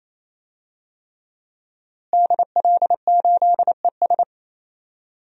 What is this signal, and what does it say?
Morse code, heard as a keyed tone, about 27 wpm, 700 Hz. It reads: DL8ES
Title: DXCC-100 Call Signs - 5 Characters - Repeated 3x - 28wpm